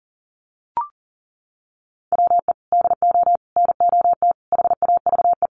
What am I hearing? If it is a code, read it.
Morse code, heard as a keyed tone, about 40 wpm, 700 Hz. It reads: WE DO NOT HAVE
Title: Sets of 4 Words 40wpm